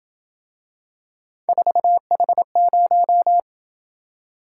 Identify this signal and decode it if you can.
Morse code, heard as a keyed tone, about 26 wpm, 700 Hz. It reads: 4H0